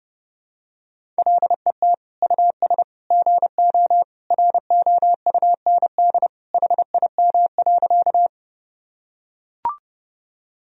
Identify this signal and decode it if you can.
Morse code, heard as a keyed tone, about 30 wpm, 700 Hz. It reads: LET US GO ROUND HIM.
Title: Sentences from Top 1000 Words - Review Part 1 - 30wpm